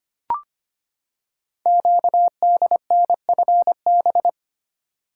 Morse code, heard as a keyed tone, about 25 wpm, 700 Hz. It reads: QDNFB
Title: Code Groups - 5 Letters - 25wpm